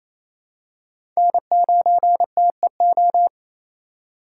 Morse code, heard as a keyed tone, about 28 wpm, 700 Hz. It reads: N9TEO
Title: USA Call Signs - 5 Characters - 2x Character Spacing - 28wpm